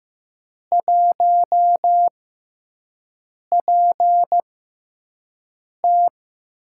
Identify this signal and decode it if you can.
Morse code, heard as a keyed tone, about 15 wpm, spaced as for 5 wpm, 700 Hz. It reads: 1PT
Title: USA Call Signs - 6x Character Spacing - 15wpm